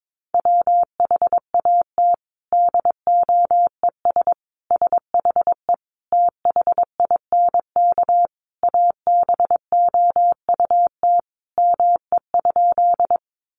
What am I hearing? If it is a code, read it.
Morse code, heard as a keyed tone, about 22 wpm, 700 Hz. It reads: WHAT DOES SHE THINK ABOUT ME?
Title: Sentences from Top 200 Words 22wpm